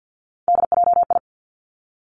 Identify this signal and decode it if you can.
Morse code, heard as a keyed tone, about 50 wpm, 700 Hz. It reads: DJI
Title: Three Challenging Letter-Number Combinations 50wpm